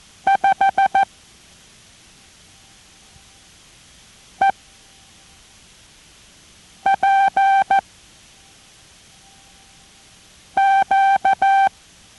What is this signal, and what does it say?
Morse code, heard as a keyed tone, about 14 wpm, 780 Hz. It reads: 5 E P Q